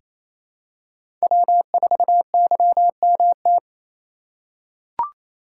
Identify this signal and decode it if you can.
Morse code, heard as a keyed tone, about 28 wpm, 700 Hz. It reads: W4YMT
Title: USA Call Signs - 5 Characters - 4x Character Spacing - 28wpm